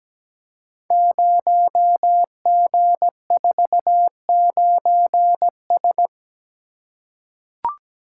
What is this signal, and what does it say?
Morse code, heard as a keyed tone, about 17 wpm, 700 Hz. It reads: 0G49S
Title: Code Groups - 5 Letter:Numbers - 17wpm